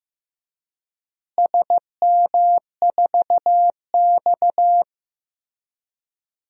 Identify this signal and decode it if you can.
Morse code, heard as a keyed tone, about 15 wpm, 700 Hz. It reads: SM4X
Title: DXCC-100 Call Signs - 4 Characters - Repeated 3x - 15wpm